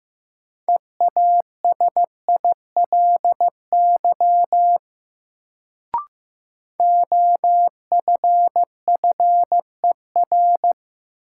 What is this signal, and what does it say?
Morse code, heard as a keyed tone, about 15 wpm, 700 Hz. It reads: EASILY OFFER
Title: Sentences from Top 900 Words - New Word Drill - 15wpm